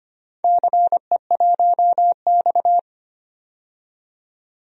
Morse code, heard as a keyed tone, about 25 wpm, 700 Hz. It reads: CE1X
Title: DXCC-100 Call Signs - 4 Characters - 25wpm